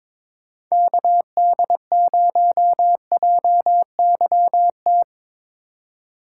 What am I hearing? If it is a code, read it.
Morse code, heard as a keyed tone, about 22 wpm, 700 Hz. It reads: KD0JYT
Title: USA Call Signs - 6 Characters - 4x Character Spacing - 22wpm